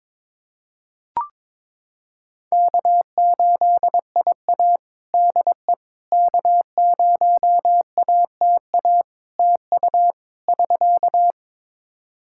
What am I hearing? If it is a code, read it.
Morse code, heard as a keyed tone, about 22 wpm, 700 Hz. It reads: K8IA DE K0ATA TU <SK>
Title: Practice QSO - Exchange 4 - Ending - 22wpm